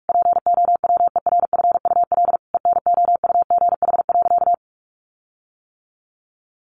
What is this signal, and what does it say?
Morse code, heard as a keyed tone, about 44 wpm, 700 Hz. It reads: POWERFUL ENOUGH.
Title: Sentences from Top 1300 Words 45wpm